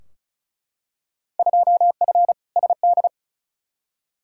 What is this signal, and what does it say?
Morse code, heard as a keyed tone, about 36 wpm, 700 Hz. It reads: 2F SD
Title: Field Day Exchanges Encore - 35wpm